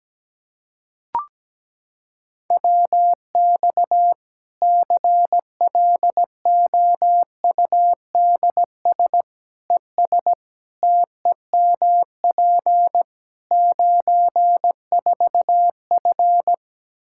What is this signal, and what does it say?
Morse code, heard as a keyed tone, about 17 wpm, 700 Hz. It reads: WX CLOUDS ES TEMP 94F
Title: Practice QSO - Exchange 2 - Info - 17wpm